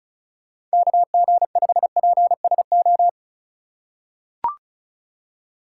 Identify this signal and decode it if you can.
Morse code, heard as a keyed tone, about 36 wpm, 700 Hz. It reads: KG5PSO
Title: USA Call Signs - 6 Characters - 3x Character Spacing - 35wpm